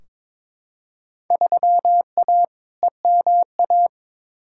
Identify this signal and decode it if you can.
Morse code, heard as a keyed tone, about 21 wpm, 700 Hz. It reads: 3A EMA
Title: Field Day Exchanges - 22wpm